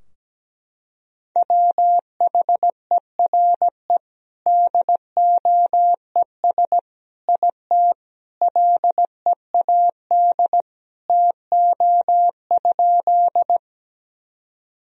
Morse code, heard as a keyed tone, about 17 wpm, 700 Hz. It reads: WHERE DOES IT LEAD TO?
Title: Sentences from Top 1000 Words - Review Part 2 - 17wpm